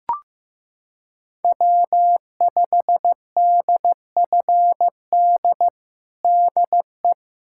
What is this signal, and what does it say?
Morse code, heard as a keyed tone, about 16 wpm, 700 Hz. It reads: W5DFD DE
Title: USA Callsign DE Callsign - 5 Characters - 15wpm